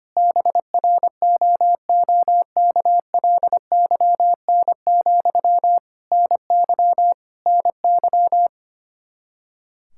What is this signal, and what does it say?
Morse code, heard as a keyed tone, about 25 wpm, 700 Hz. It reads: BROOKLYN, NY NY
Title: Practice QSO - Exchange 1 - Beginning - 25wpm